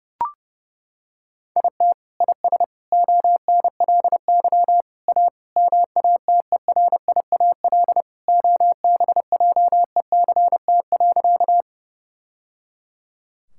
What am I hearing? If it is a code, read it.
Morse code, heard as a keyed tone, about 28 wpm, 700 Hz. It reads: IT IS ONLY A MATERIAL OBJECT.
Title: Sentences from Top 1000 Words - Review Part 5 - 30wpm